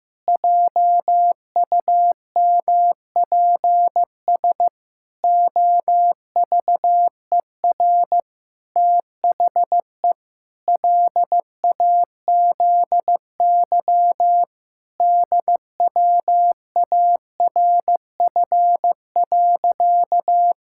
Morse code, heard as a keyed tone, about 15 wpm, 700 Hz. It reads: JUMPS OVER THE LAZY DWARF.
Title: Pangram Challenge v2 - 15wpm